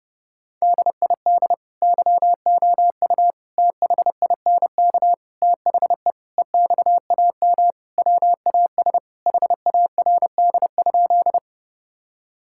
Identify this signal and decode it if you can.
Morse code, heard as a keyed tone, about 30 wpm, 700 Hz. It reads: DID YOU THINK THE EXAM WAS HARD?
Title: Sentences from Top 1000 Words - Review Part 5 - 30wpm